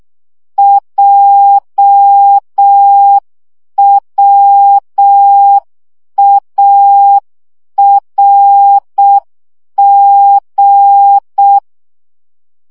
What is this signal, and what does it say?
Morse code, heard as a keyed tone, about 6 wpm, 800 Hz. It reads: JWARG
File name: morze vizsga 2011-2 5